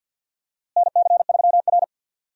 Morse code, heard as a keyed tone, about 52 wpm, 700 Hz. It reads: NC3R
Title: USA Call Signs - 4 Characters - Repeated 2x - 50wpm